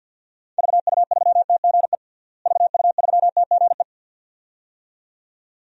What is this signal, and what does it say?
Morse code, heard as a keyed tone, about 52 wpm, 700 Hz. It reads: VU3TGE VU3TGE
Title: DXCC-100 Call Signs - 6 Characters - Repeated 2x - 50wpm